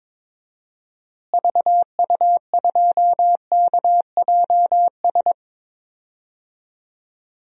Morse code, heard as a keyed tone, about 22 wpm, 700 Hz. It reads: VU2KJS
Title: DXCC-100 Call Signs - 6 Characters - Repeated 3x - 22wpm